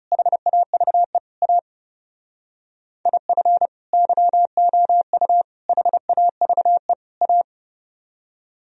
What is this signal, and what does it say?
Morse code, heard as a keyed tone, about 37 wpm, 700 Hz. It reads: HAVE A IF YOU HAVE A
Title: Speed-Racing - Sets of 4 Words 30wpm